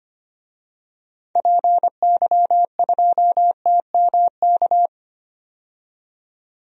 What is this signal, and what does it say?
Morse code, heard as a keyed tone, about 25 wpm, 700 Hz. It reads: PY2TMK
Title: DXCC-100 Call Signs - 6 Characters - 25wpm